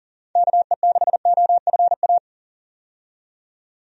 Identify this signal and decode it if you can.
Morse code, heard as a keyed tone, about 37 wpm, 700 Hz. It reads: KE6OFA
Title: USA Call Signs - 6 Characters - 40wpm